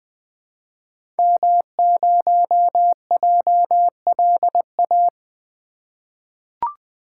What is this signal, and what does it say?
Morse code, heard as a keyed tone, about 20 wpm, 700 Hz. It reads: M0JLA
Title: 2023 SOTA Activator Call Signs - 20wpm